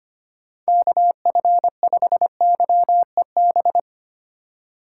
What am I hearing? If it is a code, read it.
Morse code, heard as a keyed tone, about 25 wpm, 700 Hz. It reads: KF5YEB
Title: USA Call Signs - 6x Character Spacing - 25wpm